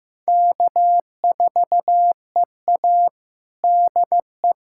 Morse code, heard as a keyed tone, about 15 wpm, 700 Hz. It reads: K4EA DE
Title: USA Callsign DE Callsign - 4 Characters - 15wpm